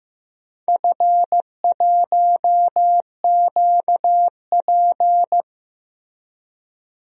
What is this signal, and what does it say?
Morse code, heard as a keyed tone, about 15 wpm, 700 Hz. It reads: F1QP
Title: DXCC-100 Call Signs - 4 Characters - Repeated 2x - 15wpm